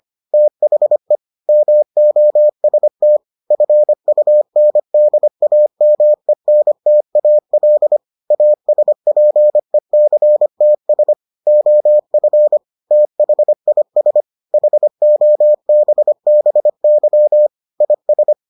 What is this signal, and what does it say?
Morse code, heard as a keyed tone, about 25 wpm, 600 Hz. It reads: THE MOST FUNDAMENTAL ASPECTS OF THIS HOBBY IS